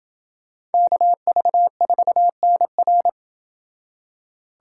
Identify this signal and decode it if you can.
Morse code, heard as a keyed tone, about 28 wpm, 700 Hz. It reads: KV4NR